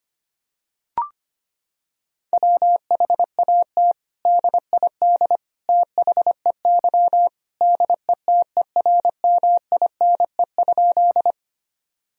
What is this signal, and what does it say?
Morse code, heard as a keyed tone, about 25 wpm, 700 Hz. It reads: WHAT DID THEY DETERMINE?
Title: Sentences from Top 1000 Words - Review Part 4 - 25wpm